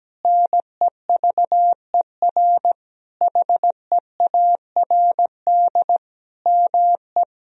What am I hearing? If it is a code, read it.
Morse code, heard as a keyed tone, about 17 wpm, 700 Hz. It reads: NEVER HEARD ME
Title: Sentences from Top 1000 Words - Review Part 2 - 17wpm